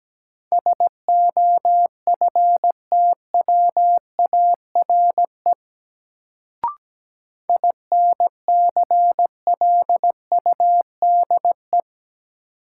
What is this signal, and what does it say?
Morse code, heard as a keyed tone, about 17 wpm, 700 Hz. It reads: SOFTWARE INCLUDE
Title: Sentences from Top 1000 Words - Review Part 3 - New Word Drill - 17wpm